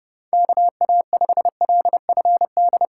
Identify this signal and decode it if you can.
Morse code, heard as a keyed tone, about 32 wpm, 700 Hz. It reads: KA5LFD
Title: USA Callsign DE Callsign - 6 Characters - 30wpm